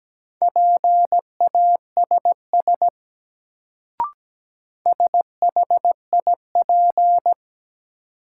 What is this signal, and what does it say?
Morse code, heard as a keyed tone, about 17 wpm, 700 Hz. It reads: PASS SHIP
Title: Sentences from Top 400 Words - New Word Drill - 17wpm